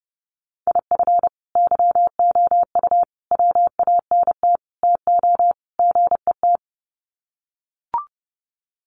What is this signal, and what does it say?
Morse code, heard as a keyed tone, about 30 wpm, 700 Hz. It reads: IF YOU WANT TO GET
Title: Sets of 5 Words Encore 30wpm